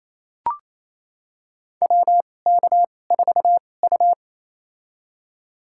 Morse code, heard as a keyed tone, about 27 wpm, spaced as for 20 wpm, 690 Hz. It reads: WK4U
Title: USA Call Signs - 4 Characters - 2x Character Spacing - 28wpm